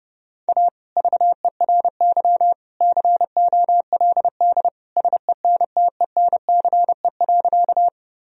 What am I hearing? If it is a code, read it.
Morse code, heard as a keyed tone, about 30 wpm, 700 Hz. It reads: A VERY COLD SENTENCE.
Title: Sentences from Top 1000 Words - Review Part 1 - 30wpm